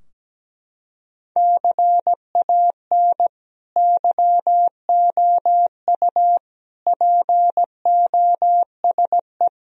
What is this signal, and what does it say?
Morse code, heard as a keyed tone, about 17 wpm, 700 Hz. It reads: CAN YOU POSE
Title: Sentences from Top 1000 Words - Review Part 2 - 17wpm